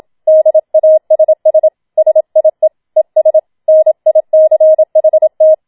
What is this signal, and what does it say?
Morse code, heard as a keyed tone, about 27 wpm, 630 Hz. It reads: DASS SIE ES NICHT
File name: Schneewittchen Teil2 27wpm